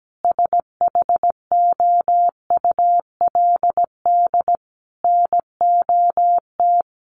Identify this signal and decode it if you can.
Morse code, heard as a keyed tone, about 17 wpm, 700 Hz. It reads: SHOULD NOT
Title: Sentences from Top 1100 Words 17wpm